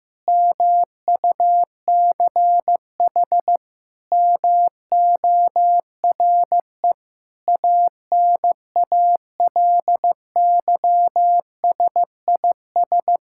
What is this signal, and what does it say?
Morse code, heard as a keyed tone, about 15 wpm, 700 Hz. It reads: MUCH MORE ANALYSIS
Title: Sentences from Top 1000 Words - Review Part 5 - 15wpm